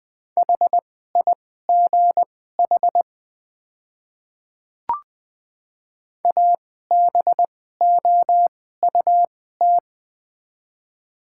Morse code, heard as a keyed tone, about 21 wpm, spaced as for 15 wpm, 700 Hz. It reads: HIGH ABOUT
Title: 2x Character Spacing - Top 200 Words - 20wpm